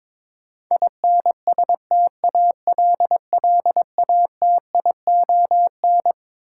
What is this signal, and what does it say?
Morse code, heard as a keyed tone, about 21 wpm, 700 Hz. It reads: INSTALLATION
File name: Twelve Letter Words - 22wpm